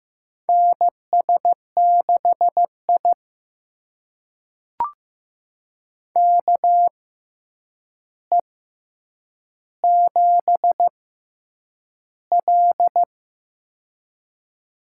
Morse code, heard as a keyed tone, about 15 wpm, spaced as for 8 wpm, 700 Hz. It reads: NS6I KE7L
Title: USA Call Signs - 6x Character Spacing - 15wpm